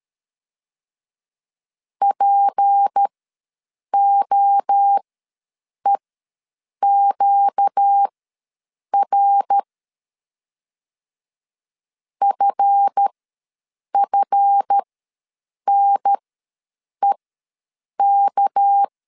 Morse code, heard as a keyed tone, about 13 wpm, spaced as for 7 wpm, 800 Hz. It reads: POEQR FFNEK
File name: LEZIONI / 7 GF9Z / 7 GF ws 12wpm 3140